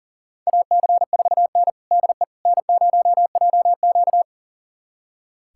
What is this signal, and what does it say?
Morse code, heard as a keyed tone, about 40 wpm, 700 Hz. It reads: AC4N DE N0JQ
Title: USA Callsign DE Callsign - 4 Characters - 40wpm